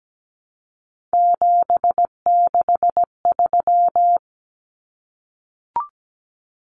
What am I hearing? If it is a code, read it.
Morse code, heard as a keyed tone, about 17 wpm, 700 Hz. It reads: 763
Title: Numbers - 3-Digit 17wpm